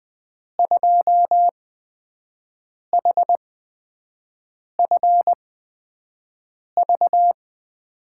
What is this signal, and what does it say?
Morse code, heard as a keyed tone, about 20 wpm, spaced as for 5 wpm, 700 Hz. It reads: 2HFV
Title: USA Call Signs - 6 Characters - 8x Character Spacing - 20wpm